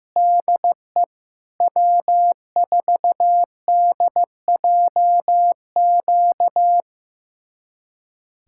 Morse code, heard as a keyed tone, about 15 wpm, 700 Hz. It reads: DE W4DJQ
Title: USA Callsign DE Callsign - 5 Characters - 15wpm